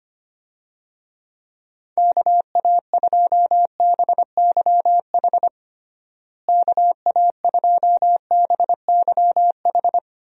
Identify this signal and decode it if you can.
Morse code, heard as a keyed tone, about 25 wpm, 700 Hz. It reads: KA2BYH KA2BYH
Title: USA Call Signs - 6 Characters - Repeated 2x - 25wpm